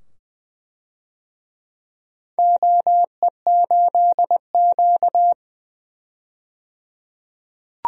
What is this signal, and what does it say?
Morse code, heard as a keyed tone, about 20 wpm, 700 Hz. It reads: OE8Q
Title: DXCC-100 Call Signs - 4 Characters - 20wpm